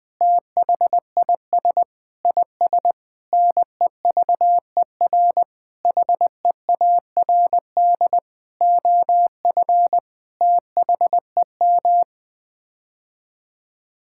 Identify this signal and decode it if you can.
Morse code, heard as a keyed tone, about 21 wpm, 700 Hz. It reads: THIS IS NEVER HEARD OF THEM
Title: Sets of 9 Words - 20wpm